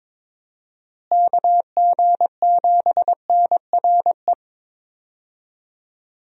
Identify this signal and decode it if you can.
Morse code, heard as a keyed tone, about 22 wpm, 700 Hz. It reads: KG7NRE
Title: USA Call Signs - 6 Characters - Repeated 3x - 22wpm